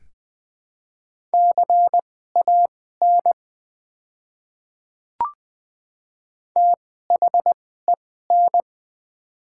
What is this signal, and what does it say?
Morse code, heard as a keyed tone, about 20 wpm, spaced as for 15 wpm, 700 Hz. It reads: CAN THEN
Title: 2x Character Spacing - Top 100 Words - 20wpm